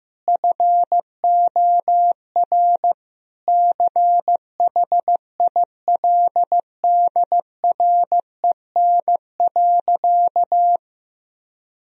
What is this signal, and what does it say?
Morse code, heard as a keyed tone, about 15 wpm, 700 Hz. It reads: FOR CHILDREN.
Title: Sentences from Top 1000 Words - Review Part 4 - 15wpm